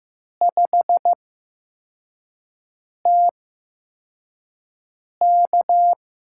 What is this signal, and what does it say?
Morse code, heard as a keyed tone, about 15 wpm, spaced as for 4 wpm, 700 Hz. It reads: 5TK
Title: USA Call Signs - 4 Characters - 8x Character Spacing - 15wpm